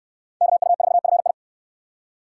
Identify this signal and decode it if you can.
Morse code, heard as a keyed tone, about 67 wpm, 700 Hz. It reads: KU4FI